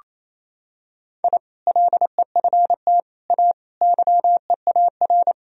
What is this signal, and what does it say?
Morse code, heard as a keyed tone, about 28 wpm, 700 Hz. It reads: I LEFT A YEAR
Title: Sentences from Top 1000 Words - Review Part 4 - 28wpm